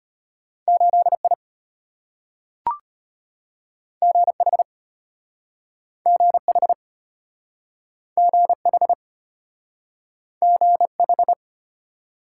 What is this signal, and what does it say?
Morse code, heard as a keyed tone, about 34 wpm, 700 Hz. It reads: 8I GH GH GH GH